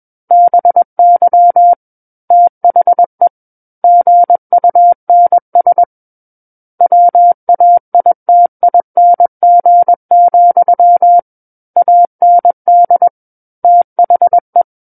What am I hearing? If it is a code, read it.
Morse code, heard as a keyed tone, about 21 wpm, 700 Hz. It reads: BY THE GUNS WAITING, AND THE
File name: Book One / 12-Chapter 12 - 21 WPM